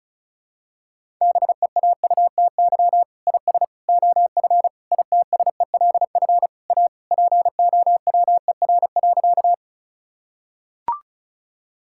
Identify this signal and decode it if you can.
Morse code, heard as a keyed tone, about 34 wpm, 700 Hz. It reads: BEAUTY IS OF ITSELF A POWER.
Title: Sentences from Top 1000 Words - Review Part 5 - 35wpm